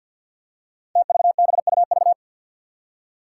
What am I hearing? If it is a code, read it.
Morse code, heard as a keyed tone, about 47 wpm, 700 Hz. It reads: TVBUV
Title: Five Challenging Letter-Number Combinations 50wpm